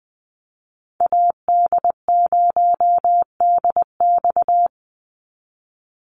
Morse code, heard as a keyed tone, about 20 wpm, 700 Hz. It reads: AD0DX
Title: Call Signs Repeated 3x - 20wpm